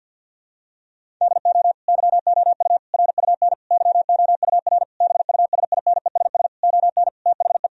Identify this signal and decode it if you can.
Morse code, heard as a keyed tone, about 50 wpm, 700 Hz. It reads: DO YOU RUN YOUR BUSINESS ON THE
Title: Sentences from Top 1000 Words - Review Part 4 - 50wpm